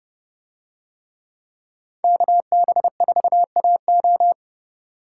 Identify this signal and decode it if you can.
Morse code, heard as a keyed tone, about 30 wpm, 700 Hz. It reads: KB4AO